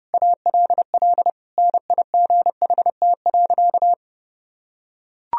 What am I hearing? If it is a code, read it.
Morse code, heard as a keyed tone, about 30 wpm, 700 Hz. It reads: ALL NIGHT.
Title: Sentences from Top 300 Words and Abbreviations 30wpm